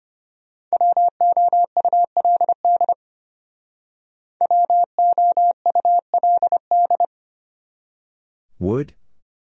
Morse code, heard as a keyed tone, about 33 wpm, 700 Hz. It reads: WOULD WOULD